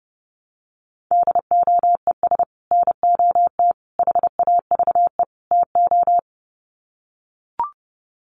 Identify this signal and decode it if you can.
Morse code, heard as a keyed tone, about 30 wpm, 700 Hz. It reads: DOES NOT HAVE TO
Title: Sets of 4 Words 30wpm